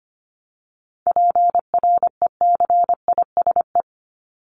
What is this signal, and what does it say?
Morse code, heard as a keyed tone, about 25 wpm, 700 Hz. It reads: PRECISE
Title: LY Suffix - Top 100 Words with Base Words 25wpm